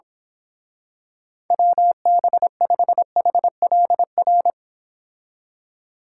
Morse code, heard as a keyed tone, about 26 wpm, 700 Hz. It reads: WB5HLR
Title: Speed Racing - USA 2x3 Call Signs 22wpm